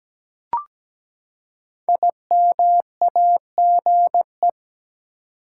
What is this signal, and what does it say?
Morse code, heard as a keyed tone, about 17 wpm, 700 Hz. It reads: IMAGE
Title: Five Letter Words - 17wpm